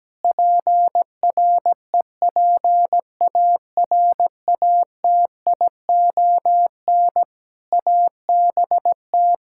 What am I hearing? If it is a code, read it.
Morse code, heard as a keyed tone, about 17 wpm, 700 Hz. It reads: PREPARATION ABT
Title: K1USN SST QSO Practice 17wpm